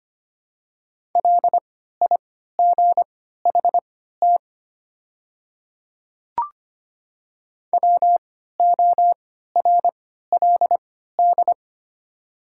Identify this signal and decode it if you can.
Morse code, heard as a keyed tone, about 25 wpm, spaced as for 14 wpm, 700 Hz. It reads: LIGHT WORLD
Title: 3x Character Spacing - Top 300 Words - 25wpm